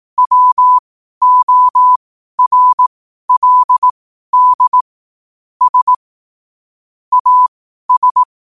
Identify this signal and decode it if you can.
Morse code, heard as a keyed tone, about 18 wpm, spaced as for 12 wpm, 1000 Hz. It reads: WORLD S AS
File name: 12wpm